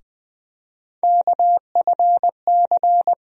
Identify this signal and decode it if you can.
Morse code, heard as a keyed tone, about 20 wpm, 700 Hz. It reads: KFC
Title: USA Call Sign Postfixes - 3 Letters 20wpm